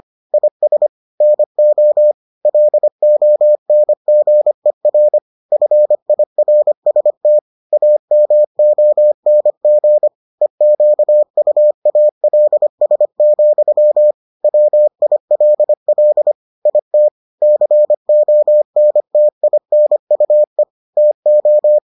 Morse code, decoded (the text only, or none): IS NO LONGER FIRST AMONG EQUALS, WILL IT CONTINUE TO